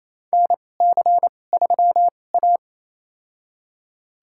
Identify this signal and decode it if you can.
Morse code, heard as a keyed tone, about 27 wpm, spaced as for 20 wpm, 700 Hz. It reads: NC3A
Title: USA Call Signs - 4 Characters - 2x Character Spacing - 28wpm